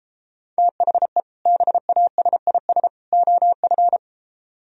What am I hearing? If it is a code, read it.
Morse code, heard as a keyed tone, about 34 wpm, 700 Hz. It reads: THE BASIS OF